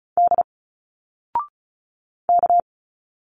Morse code, heard as a keyed tone, about 36 wpm, 700 Hz. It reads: D K
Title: Single Letters 35wpm